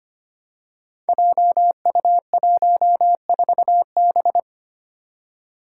Morse code, heard as a keyed tone, about 25 wpm, 700 Hz. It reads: JU14B
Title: Five Challenging Letter-Number Combinations 25wpm